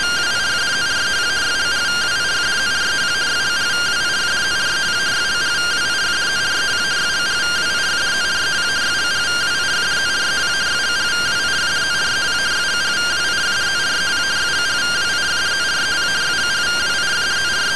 Two-tone teleprinter signal, about 170 Hz shift.